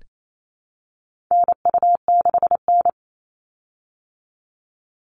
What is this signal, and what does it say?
Morse code, heard as a keyed tone, about 28 wpm, 700 Hz. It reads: NU6N
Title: Call Signs Repeated 3x - 28wpm